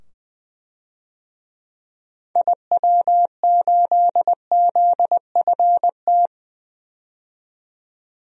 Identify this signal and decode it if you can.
Morse code, heard as a keyed tone, about 20 wpm, 700 Hz. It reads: IW8ZFT
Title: DXCC-100 Call Signs - 6 Characters - 20wpm